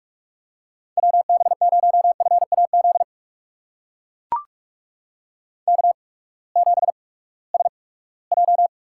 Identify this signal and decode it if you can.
Morse code, heard as a keyed tone, about 44 wpm, spaced as for 22 wpm, 700 Hz. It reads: WB0FAZ K7SJ
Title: USA Call Signs - 8x Character Spacing - 45wpm